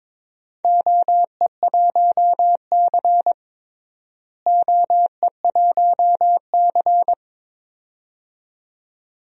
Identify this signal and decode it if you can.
Morse code, heard as a keyed tone, about 21 wpm, 700 Hz. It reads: OE1C OE1C
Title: DXCC-100 Call Signs - 4 Characters - Repeated 2x - 22wpm